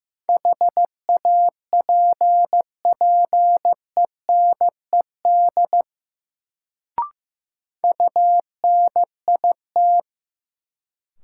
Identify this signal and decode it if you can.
Morse code, heard as a keyed tone, about 15 wpm, 700 Hz. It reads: HAPPENED UNIT
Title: Sentences from Top 1000 Words - Review Part 2 - New Word Drill - 15wpm